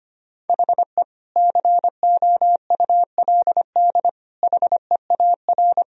Morse code, heard as a keyed tone, about 25 wpm, 700 Hz. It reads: HE COULD HEAR
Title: Sentences from Top 1000 Words - Review Part 1 - 25wpm